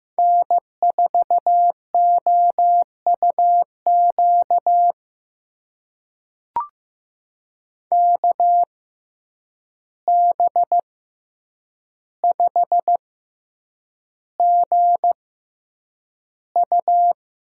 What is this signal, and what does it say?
Morse code, heard as a keyed tone, about 15 wpm, spaced as for 8 wpm, 700 Hz. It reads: N4OUQ KB5GU
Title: USA Call Signs - 5 Characters - 6x Character Spacing - 15wpm